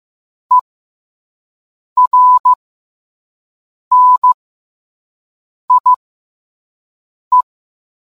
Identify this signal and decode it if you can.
Morse code, heard as a keyed tone, about 15 wpm, spaced as for 5 wpm, 1000 Hz. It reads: ERNIE